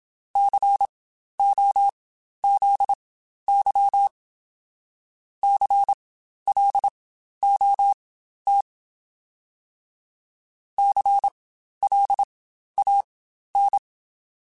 Morse code, heard as a keyed tone, about 27 wpm, spaced as for 12 wpm, 800 Hz. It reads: COZY CLOT CLAN